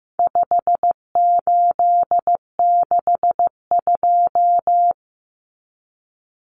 Morse code, heard as a keyed tone, about 15 wpm, 700 Hz. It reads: 5862
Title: Common Cut Numbers - 4-Digit 15wpm